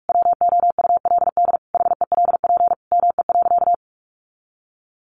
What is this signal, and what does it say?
Morse code, heard as a keyed tone, about 44 wpm, 700 Hz. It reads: WOULD HELP ME.
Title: Sentences from Top 1300 Words 45wpm